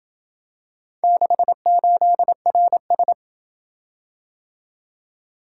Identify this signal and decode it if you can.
Morse code, heard as a keyed tone, about 26 wpm, 700 Hz. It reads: 68RS